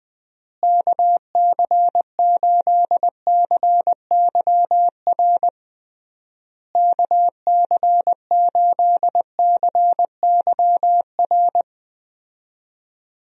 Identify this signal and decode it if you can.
Morse code, heard as a keyed tone, about 20 wpm, 700 Hz. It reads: KC8CYR KC8CYR
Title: USA Call Signs - 6 Characters - Repeated 3x - 20wpm